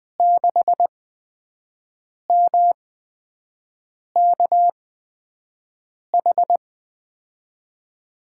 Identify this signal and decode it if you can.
Morse code, heard as a keyed tone, about 20 wpm, spaced as for 5 wpm, 700 Hz. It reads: 6MKH
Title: USA Call Signs - 6 Characters - 8x Character Spacing - 20wpm